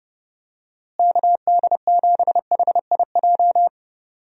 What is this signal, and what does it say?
Morse code, heard as a keyed tone, about 30 wpm, 700 Hz. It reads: KD7HIJ
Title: USA Call Signs - 6 Characters - Repeated 2x - 30wpm